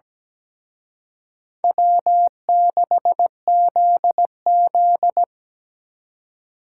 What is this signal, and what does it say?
Morse code, heard as a keyed tone, about 20 wpm, 700 Hz. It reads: W6ZZ
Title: Speed Racing - USA 1x2 Call Signs 17wpm